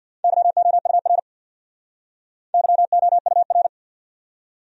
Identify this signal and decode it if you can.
Morse code, heard as a keyed tone, about 65 wpm, 700 Hz. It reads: YOUR YOUR